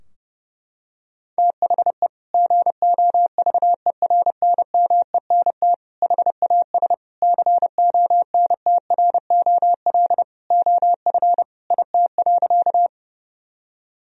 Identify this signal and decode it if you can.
Morse code, heard as a keyed tone, about 30 wpm, 700 Hz. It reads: THE GOVERNMENT HAS CONTROL OF IT.
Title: Sentences from Top 1000 Words - Review Part 3 - 30wpm